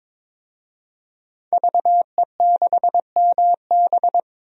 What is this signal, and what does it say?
Morse code, heard as a keyed tone, about 21 wpm, 700 Hz. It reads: VE6MB
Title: DXCC-100 Call Signs - 5 Characters - 22wpm